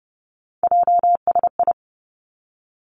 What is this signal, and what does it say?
Morse code, heard as a keyed tone, about 30 wpm, 700 Hz. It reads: JSI
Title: Three Challenging Letter-Number Combinations 30wpm